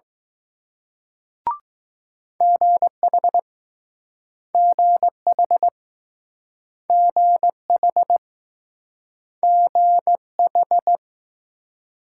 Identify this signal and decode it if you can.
Morse code, heard as a keyed tone, about 20 wpm, 700 Hz. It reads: GH GH GH GH